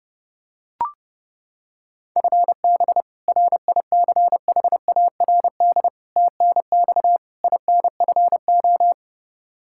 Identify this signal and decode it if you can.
Morse code, heard as a keyed tone, about 30 wpm, 700 Hz. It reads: FB RICHARD TNX INFO
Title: Practice QSO - Exchange 3 - More Info - 30wpm